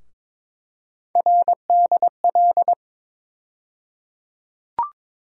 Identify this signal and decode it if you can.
Morse code, heard as a keyed tone, about 22 wpm, 700 Hz. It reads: RDL
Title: USA Call Sign Postfixes - 3 Letters 22wpm